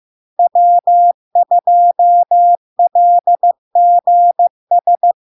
Morse code, heard as a keyed tone, about 15 wpm, 700 Hz. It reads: W2LGS
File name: LARK-FD-2022-Log-in-CW-15wpm-700-Hz